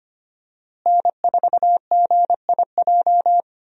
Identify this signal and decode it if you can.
Morse code, heard as a keyed tone, about 25 wpm, 700 Hz. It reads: N4GIJ
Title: USA Call Signs - 5 Characters - Repeated 2x - 25wpm